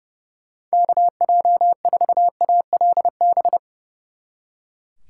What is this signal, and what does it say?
Morse code, heard as a keyed tone, about 37 wpm, 700 Hz. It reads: KJ4ALB
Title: Speed Racing - USA 2x3 Call Signs 30wpm